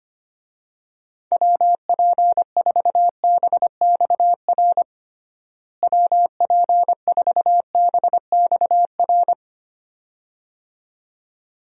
Morse code, heard as a keyed tone, about 25 wpm, 700 Hz. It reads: WP4BXR WP4BXR
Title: USA Call Signs - 6 Characters - Repeated 2x - 25wpm